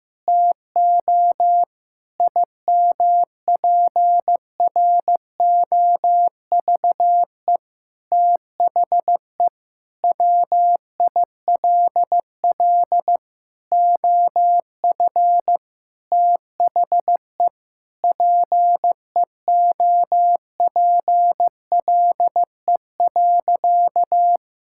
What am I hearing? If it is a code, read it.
Morse code, heard as a keyed tone, about 16 wpm, 700 Hz. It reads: TO IMPROVE THE WILL OF THE PEOPLE.
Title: Sentences from Top 1000 Words - Review Part 5 - 15wpm